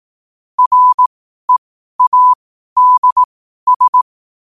Morse code, heard as a keyed tone, about 18 wpm, spaced as for 12 wpm, 1000 Hz. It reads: READS
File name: qotdfiles / 12wpm